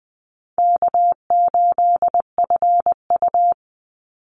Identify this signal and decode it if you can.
Morse code, heard as a keyed tone, about 20 wpm, 700 Hz. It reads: K8FU
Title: Easy Call Signs Repeated 3x - 20wpm